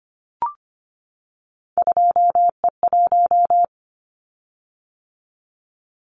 Morse code, heard as a keyed tone, about 25 wpm, 700 Hz. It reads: 2E1
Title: Callsign Prefixes for Top 25 DXCC - 25wpm